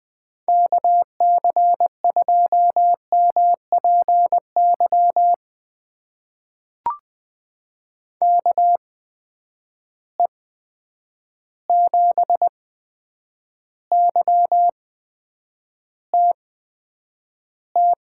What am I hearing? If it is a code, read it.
Morse code, heard as a keyed tone, about 20 wpm, spaced as for 9 wpm, 700 Hz. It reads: KC2MPY KE7YTT
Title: USA Call Signs - 6 Characters - 8x Character Spacing - 20wpm